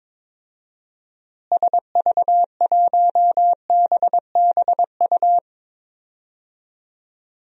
Morse code, heard as a keyed tone, about 21 wpm, 700 Hz. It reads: SV1BBU
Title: DXCC-100 Call Signs - 6 Characters - Repeated 2x - 22wpm